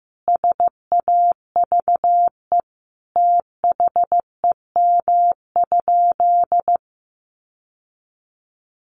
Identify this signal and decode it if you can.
Morse code, heard as a keyed tone, about 15 wpm, 700 Hz. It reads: SAVE THEM?
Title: Sentences from Top 1100 Words 15wpm